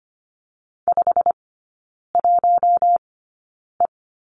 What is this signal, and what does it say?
Morse code, heard as a keyed tone, about 25 wpm, spaced as for 9 wpm, 700 Hz. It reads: 51E
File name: Lesson 15e - Number 5 - Three Characters 25wpm